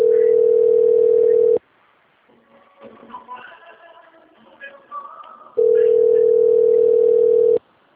Ringback tone (North America), 440 + 480 Hz.
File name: out- - - - - .54204